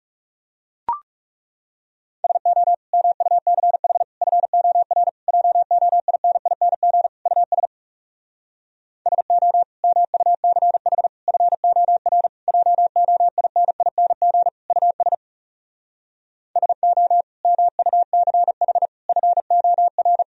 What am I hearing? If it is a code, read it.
Morse code, heard as a keyed tone, about 40 wpm, 700 Hz. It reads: SO MUCH FOR JOINING US SO MUCH FOR JOINING US SO MUCH FOR